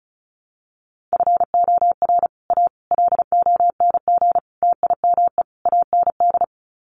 Morse code, heard as a keyed tone, about 34 wpm, 700 Hz. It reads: FOR A LONG TIME AND
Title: Sets of 5 Words Encore2 35wpm